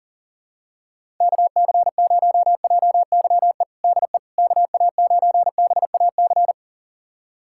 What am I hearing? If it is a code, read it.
Morse code, heard as a keyed tone, about 40 wpm, 700 Hz. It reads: KC0JYE DE KA9BAC
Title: USA Callsign DE Callsign - 6 Characters - 40wpm